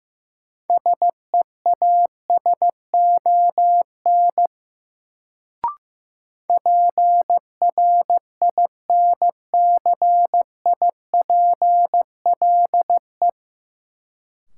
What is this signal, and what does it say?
Morse code, heard as a keyed tone, about 15 wpm, 700 Hz. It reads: SEASON PRINCIPLE
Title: Sentences from Top 1400 Words - New Word Drill - 15wpm